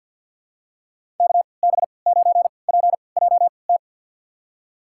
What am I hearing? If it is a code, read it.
Morse code, heard as a keyed tone, about 52 wpm, spaced as for 29 wpm, 700 Hz. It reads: KB9PJT
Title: USA Call Signs - 6 Characters - 3x Character Spacing - 50wpm